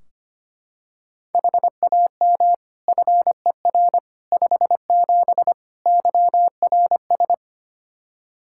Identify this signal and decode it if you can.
Morse code, heard as a keyed tone, about 25 wpm, 700 Hz. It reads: HAM FER 57 YRS
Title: K1USN SST QSO Practice 25wpm